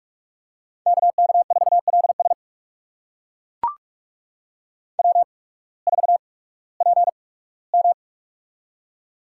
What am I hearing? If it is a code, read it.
Morse code, heard as a keyed tone, about 42 wpm, spaced as for 21 wpm, 700 Hz. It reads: KK4LS W4PM
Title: USA Call Signs - 8x Character Spacing - 45wpm